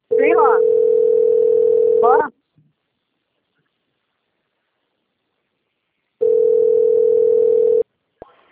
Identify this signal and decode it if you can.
Ringback tone (North America), 440 + 480 Hz, with speech.
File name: out- - - - - .917